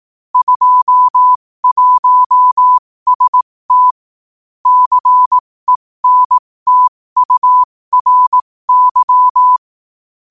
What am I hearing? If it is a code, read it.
Morse code, heard as a keyed tone, about 18 wpm, spaced as for 15 wpm, 1000 Hz. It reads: 21ST CENTURY
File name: qotdfiles / 15wpm